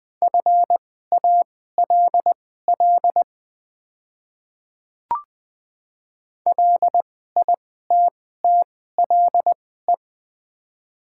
Morse code, heard as a keyed tone, about 20 wpm, spaced as for 14 wpm, 700 Hz. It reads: FALL LITTLE
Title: 2x Character Spacing - Top 500 Words - 20wpm